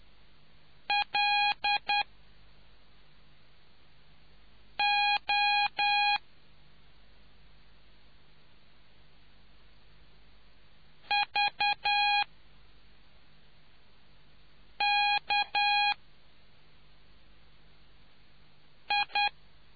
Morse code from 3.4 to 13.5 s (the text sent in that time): O V